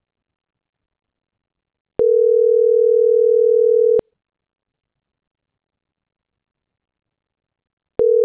Ringback tone (North America), 440 + 480 Hz.